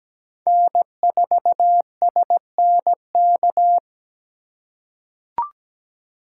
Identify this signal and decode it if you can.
Morse code, heard as a keyed tone, about 17 wpm, 700 Hz. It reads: N4SNK
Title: USA Call Signs - 5 Characters - 2x Character Spacing - 17wpm